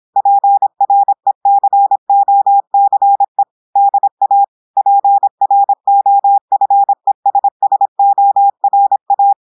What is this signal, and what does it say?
Morse code, heard as a keyed tone, about 26 wpm, 800 Hz. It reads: PRECOCE DA PROFESSORA
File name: audios / portugais / portugais texte en clair 1 vitesse 26 mots minutes